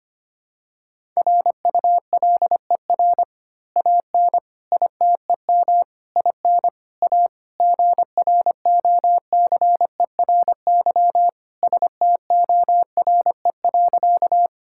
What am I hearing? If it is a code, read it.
Morse code, heard as a keyed tone, about 25 wpm, 700 Hz. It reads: RULER AN ITEM IN A GROCERY STORE.
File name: practice / 25wpm / Easy Words with Hints - 25wpm